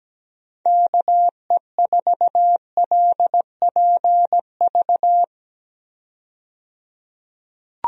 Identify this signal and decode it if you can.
Morse code, heard as a keyed tone, about 17 wpm, 700 Hz. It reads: KE4LPV
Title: USA Call Signs - 6 Characters - 17wpm